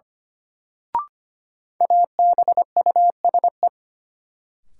Morse code, heard as a keyed tone, about 25 wpm, 700 Hz. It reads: ABUSE
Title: Sentences from Top 1300 Words - New Word Drill - 25wpm